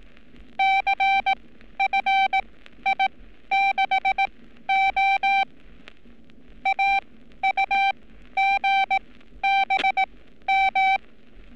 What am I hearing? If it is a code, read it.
Morse code, heard as a keyed tone, about 18 wpm, spaced as for 12 wpm, 780 Hz. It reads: CFI6O AUGBM
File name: training / NAVPERS 11224-11235 / NAVPERS 11227 RA RB